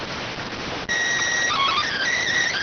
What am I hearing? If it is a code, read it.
SSTV (slow-scan television), PD 290.